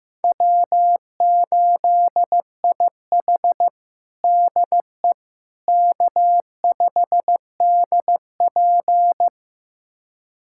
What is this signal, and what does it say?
Morse code, heard as a keyed tone, about 15 wpm, 700 Hz. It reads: W8IH DE K5DP
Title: USA Callsign DE Callsign - 4 Characters - 15wpm